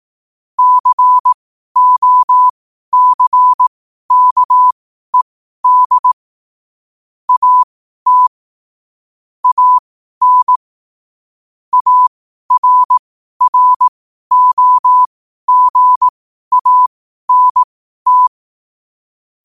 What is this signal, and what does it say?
Morse code, heard as a keyed tone, about 18 wpm, spaced as for 12 wpm, 1000 Hz. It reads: COCKED AT AN ARROGANT
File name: qotdfiles / 12wpm